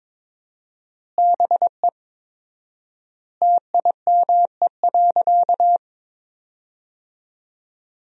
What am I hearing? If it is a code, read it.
Morse code, heard as a keyed tone, about 22 wpm, 700 Hz. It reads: BE TIME.
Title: 4x Word-Spacing - Sentences from Top 200 Words - 22wpm